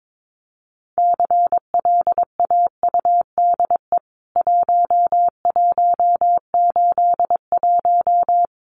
Morse code, heard as a keyed tone, about 22 wpm, 700 Hz. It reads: CLAUDE 1181
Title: CWT Exchange 22wpm